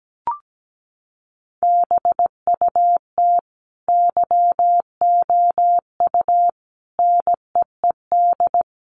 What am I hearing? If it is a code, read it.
Morse code, heard as a keyed tone, about 17 wpm, 700 Hz. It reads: BUT YOU NEED
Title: Sentences from Top 1300 Words 17wpm